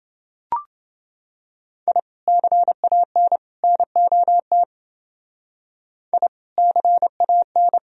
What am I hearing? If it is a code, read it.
Morse code, heard as a keyed tone, about 30 wpm, 700 Hz. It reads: I CAN NOT I CAN